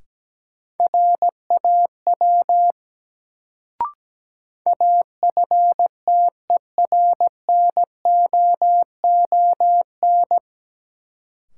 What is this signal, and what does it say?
Morse code, heard as a keyed tone, about 17 wpm, 700 Hz. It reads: RAW AFTERNOON
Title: Sentences from Top 1500 Words - New Word Drill Alt - 17wpm